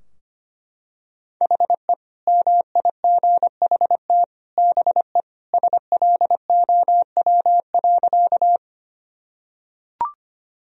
Morse code, heard as a keyed tone, about 25 wpm, 700 Hz. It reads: HE MIGHT BE SLOW.
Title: Sentences from Top 1000 Words - Review Part 2 - 25wpm